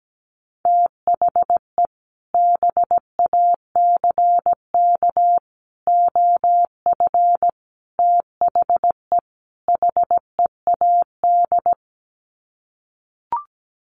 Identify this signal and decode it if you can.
Morse code, heard as a keyed tone, about 17 wpm, 700 Hz. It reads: THE BACK OF THE HEAD
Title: Sets of 5 Words Encore2 17wpm